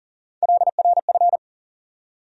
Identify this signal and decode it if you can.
Morse code, heard as a keyed tone, about 40 wpm, 700 Hz. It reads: LRF
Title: USA Call Sign Postfixes - 3 Letters 40wpm